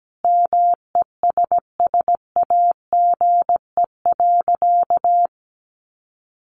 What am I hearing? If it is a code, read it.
Morse code, heard as a keyed tone, about 17 wpm, 700 Hz. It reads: MESSAGE.
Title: Sentences from Top 1100 Words 17wpm